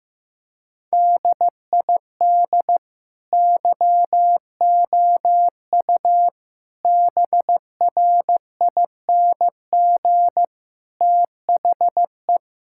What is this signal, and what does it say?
Morse code, heard as a keyed tone, about 15 wpm, 700 Hz. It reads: DID YOU BRING THE
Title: Sentences from Top 1000 Words - Review Part 6 - 15wpm